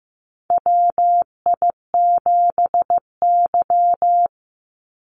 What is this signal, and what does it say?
Morse code, heard as a keyed tone, about 15 wpm, 700 Hz. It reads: WI7Y
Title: USA Call Signs 15wpm